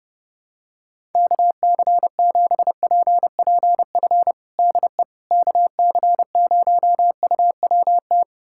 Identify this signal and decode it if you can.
Morse code, heard as a keyed tone, about 30 wpm, 700 Hz. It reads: KC7PPF DE KC0UWT
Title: USA Callsign DE Callsign - 6 Characters - 30wpm